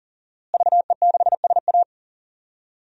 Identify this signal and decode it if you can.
Morse code, heard as a keyed tone, about 37 wpm, 700 Hz. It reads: VE6SA
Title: DXCC-100 Call Signs - 5 Characters - Repeated 3x - 40wpm